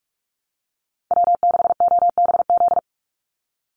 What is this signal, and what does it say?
Morse code, heard as a keyed tone, about 44 wpm, 700 Hz. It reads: W6OBZ
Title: USA Call Signs 45wpm